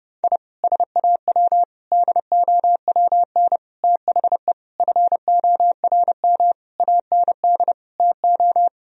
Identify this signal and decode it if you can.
Morse code, heard as a keyed tone, about 32 wpm, 700 Hz. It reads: I SAW DOWN THE FORM AND TO
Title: Sentences from Top 1000 Words - Review Part 1 - 30wpm